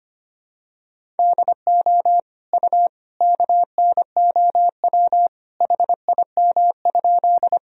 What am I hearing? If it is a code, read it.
Morse code, heard as a keyed tone, about 25 wpm, 700 Hz. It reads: DO U KNOW HIM?
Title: Sentences from Top 100 Words and Abbreviations 25wpm